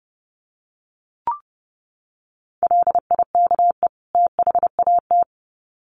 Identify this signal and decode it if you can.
Morse code, heard as a keyed tone, about 30 wpm, 700 Hz. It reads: LIKE THAT
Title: Sets of 2 Words 30wpm